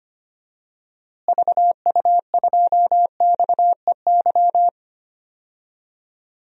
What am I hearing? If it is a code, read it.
Morse code, heard as a keyed tone, about 25 wpm, 700 Hz. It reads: VU2XEY
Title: DXCC-100 Call Signs - 6 Characters - Repeated 2x - 25wpm